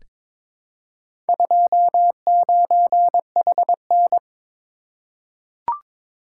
Morse code, heard as a keyed tone, about 22 wpm, 700 Hz. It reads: 29HN
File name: practice / 22wpm / Four Letter-Number Combinations - 22wpm